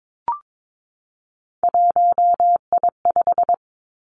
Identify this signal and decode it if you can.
Morse code, heard as a keyed tone, about 22 wpm, 700 Hz. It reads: 1I5
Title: Three Challenging Letter-Number Combinations 22wpm